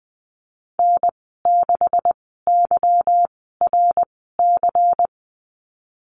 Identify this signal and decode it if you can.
Morse code, heard as a keyed tone, about 20 wpm, spaced as for 15 wpm, 700 Hz. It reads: N6YRC
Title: 147 Lesson 24g - Letter y - Call Signs 20wpm